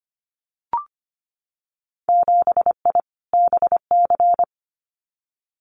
Morse code, heard as a keyed tone, about 25 wpm, 700 Hz. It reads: 7I BC
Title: Winter Field Day Exchanges Encore 25wpm